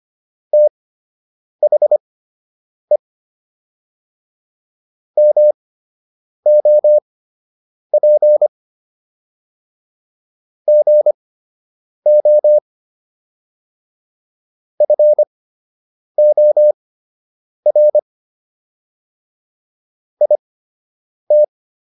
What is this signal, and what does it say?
Morse code, heard as a keyed tone, about 25 wpm, spaced as for 8 wpm, 600 Hz. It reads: THE MOP GO FOR IT